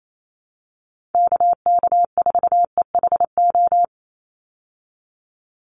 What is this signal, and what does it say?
Morse code, heard as a keyed tone, about 28 wpm, 700 Hz. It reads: KK4EHO
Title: USA Call Signs 28wpm